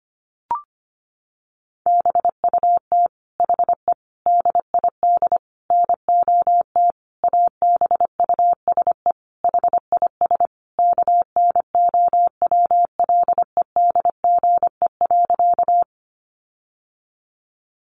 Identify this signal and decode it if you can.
Morse code, heard as a keyed tone, about 25 wpm, 700 Hz. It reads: BUT HE DID NOT ABUSE HIS KNOWLEDGE.
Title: Sentences from Top 1300 Words 25wpm